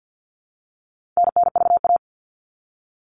Morse code, heard as a keyed tone, about 50 wpm, 700 Hz. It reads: NNVA
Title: All Cut Numbers - 4-Digit 50wpm